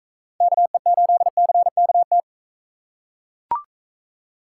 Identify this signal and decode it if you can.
Morse code, heard as a keyed tone, about 42 wpm, 700 Hz. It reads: KE8CKT